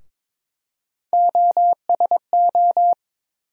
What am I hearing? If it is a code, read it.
Morse code, heard as a keyed tone, about 22 wpm, 700 Hz. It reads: OSO
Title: USA Call Sign Postfixes - 3 Letters 22wpm